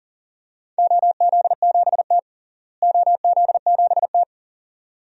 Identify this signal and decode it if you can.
Morse code, heard as a keyed tone, about 40 wpm, 700 Hz. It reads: OZ7T OZ7T
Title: DXCC-100 Call Signs - 4 Characters - Repeated 2x - 40wpm